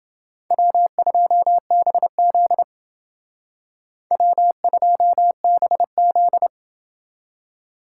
Morse code, heard as a keyed tone, about 28 wpm, 700 Hz. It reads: W2BZ W2BZ